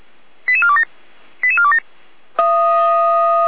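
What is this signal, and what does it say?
Five-tone selective call (ZVEI), heard as quick runs of five tones, tones of about 70 ms. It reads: 88427 88427
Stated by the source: ZVEI five-tone call received on SDR.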